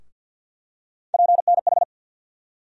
Morse code, heard as a keyed tone, about 53 wpm, 700 Hz. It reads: PNH
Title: USA Call Sign Postfixes - 3 Letters 50wpm